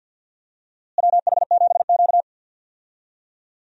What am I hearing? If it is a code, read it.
Morse code, heard as a keyed tone, about 52 wpm, 700 Hz. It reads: WH7Q
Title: USA Call Signs - 4 Characters - Repeated 3x - 50wpm